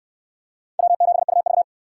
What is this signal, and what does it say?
Morse code, heard as a keyed tone, about 59 wpm, 700 Hz. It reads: W7UV